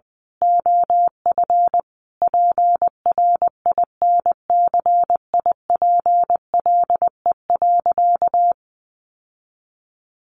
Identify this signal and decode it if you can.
Morse code, heard as a keyed tone, about 20 wpm, 700 Hz. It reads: OF PRINCIPLE.
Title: Sentences from Top 1400 Words 20wpm